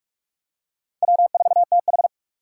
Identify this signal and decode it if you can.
Morse code, heard as a keyed tone, about 42 wpm, 700 Hz. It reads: W4TH